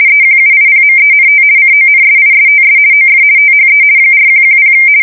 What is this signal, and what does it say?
RTTY, about 50 baud. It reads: EQUIPMENT CONSISTS OF THE 4110
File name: fsk170-50B / teletype-corp